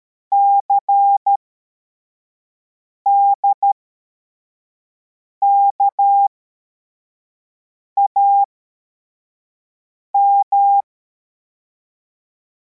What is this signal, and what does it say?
Morse code, heard as a keyed tone, about 13 wpm, spaced as for 4 wpm, 800 Hz. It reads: CDKAM